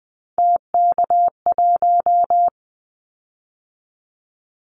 Morse code, heard as a keyed tone, about 20 wpm, 700 Hz. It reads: TK1
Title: Callsign Prefixes for Top 100 DXCC - 20wpm